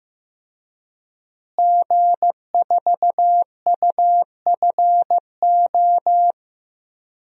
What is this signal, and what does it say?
Morse code, heard as a keyed tone, about 16 wpm, 700 Hz. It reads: G4UFO
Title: DXCC-100 Call Signs - 5 Characters - Repeated 2x - 15wpm